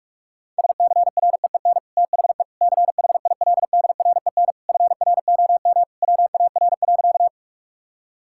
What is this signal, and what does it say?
Morse code, heard as a keyed tone, about 46 wpm, 700 Hz. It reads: SCREEN THE CHILDREN FROM WAR.
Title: Sentences from Top 1000 Words - Review Part 6 - 45wpm